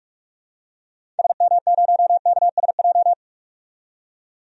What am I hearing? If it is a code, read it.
Morse code, heard as a keyed tone, about 42 wpm, 700 Hz. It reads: SM0KSJ
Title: DXCC-100 Call Signs - 6 Characters - 45wpm